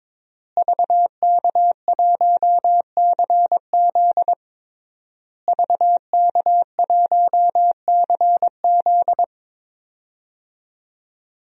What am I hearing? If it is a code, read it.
Morse code, heard as a keyed tone, about 22 wpm, 700 Hz. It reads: VK1CZ VK1CZ
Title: DXCC-100 Call Signs - 5 Characters - Repeated 3x - 22wpm